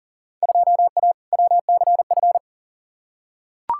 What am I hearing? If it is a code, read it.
Morse code, heard as a keyed tone, about 40 wpm, 700 Hz. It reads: 2A WCF
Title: Field Day Exchanges Encore - 40wpm